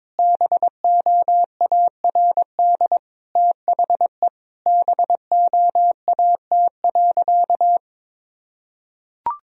Morse code, heard as a keyed tone, about 22 wpm, 700 Hz. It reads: BOARD THE BOAT.
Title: Sentences from Top 1000 Words - Review Part 4 - 22wpm